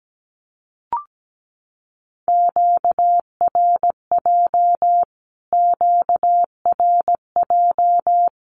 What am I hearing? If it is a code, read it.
Morse code, heard as a keyed tone, about 17 wpm, 700 Hz. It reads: QRJ QRJ
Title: Q-Codes Repeated 3x 17wpm